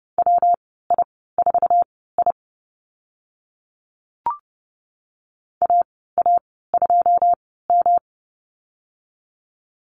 Morse code, heard as a keyed tone, about 30 wpm, spaced as for 17 wpm, 700 Hz. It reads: WI4I AA2M
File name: Lesson 19g - Letter w - Call Signs 30wpm